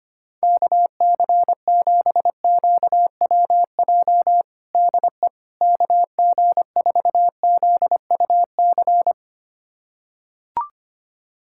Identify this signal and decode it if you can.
Morse code, heard as a keyed tone, about 25 wpm, 700 Hz. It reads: KC7QWJ DE KG4ZUC
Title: USA Callsign DE Callsign - 6 Characters - 25wpm